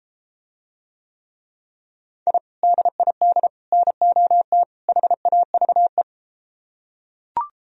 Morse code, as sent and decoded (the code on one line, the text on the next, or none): .. / -.. .. -.. / -. --- - / .... .- ...- .
I DID NOT HAVE